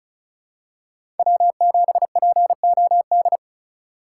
Morse code, heard as a keyed tone, about 36 wpm, 700 Hz. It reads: W7POD
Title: USA Call Signs - 5 Characters - 3x Character Spacing - 35wpm